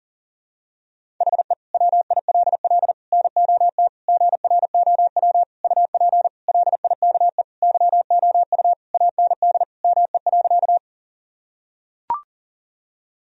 Morse code, heard as a keyed tone, about 40 wpm, 700 Hz. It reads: HE WILL NOT GROW UP LIKE YOU AND ME.
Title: Sentences from Top 1000 Words - Review Part 1 - 40wpm